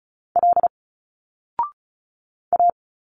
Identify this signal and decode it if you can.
Morse code, heard as a keyed tone, about 35 wpm, 700 Hz. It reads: L A
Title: Single Letters 35wpm